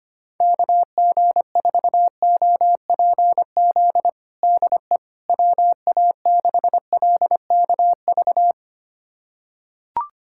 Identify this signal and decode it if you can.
Morse code, heard as a keyed tone, about 25 wpm, 700 Hz. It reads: KG4OPZ DE WA6LKV
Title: USA Callsign DE Callsign - 6 Characters - 25wpm